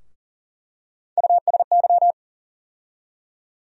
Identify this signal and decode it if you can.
Morse code, heard as a keyed tone, about 40 wpm, 700 Hz. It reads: USY
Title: USA Call Sign Postfixes - 3 Letters 40wpm